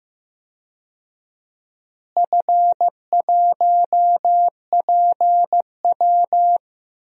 Morse code, heard as a keyed tone, about 16 wpm, 700 Hz. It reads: F1PW
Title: DXCC-100 Call Signs - 4 Characters - Repeated 2x - 15wpm